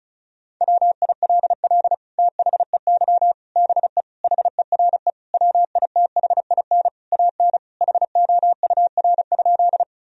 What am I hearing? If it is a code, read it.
Morse code, heard as a keyed tone, about 34 wpm, 700 Hz. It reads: WILL THEY BE HERE WITHIN AN HOUR?
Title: Sentences from Top 1000 Words - Review Part 4 - 35wpm